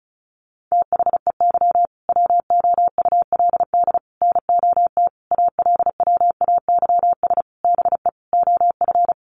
Morse code, heard as a keyed tone, about 34 wpm, 700 Hz. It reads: THEY WOULD NOT ALWAYS BE OF
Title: Sentences from Top 1400 Words 35wpm